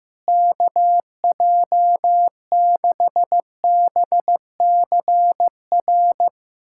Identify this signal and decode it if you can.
Morse code, heard as a keyed tone, about 16 wpm, 700 Hz. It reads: KJ6BCR
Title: USA Callsign DE Callsign - 6 Characters - 15wpm